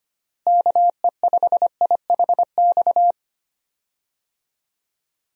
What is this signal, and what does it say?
Morse code, heard as a keyed tone, about 25 wpm, 700 Hz. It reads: KE5IHX
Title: USA Call Signs - 6 Characters - Repeated 3x - 25wpm